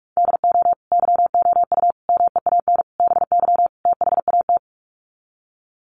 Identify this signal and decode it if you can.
Morse code, heard as a keyed tone, about 44 wpm, 700 Hz. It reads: DO YOU MEAN BY THAT
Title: Sets of 5 Words Encore2 45wpm